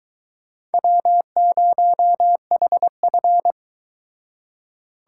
Morse code, heard as a keyed tone, about 22 wpm, 700 Hz. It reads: W0HF